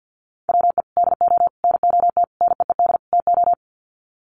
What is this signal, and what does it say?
Morse code, heard as a keyed tone, about 50 wpm, 700 Hz. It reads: WE DO NOT NEED TO
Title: Sets of 5 Words Encore 50wpm